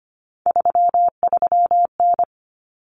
Morse code, heard as a keyed tone, about 25 wpm, 700 Hz. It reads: 33N
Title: Common Cut Numbers - 3-Digit 25wpm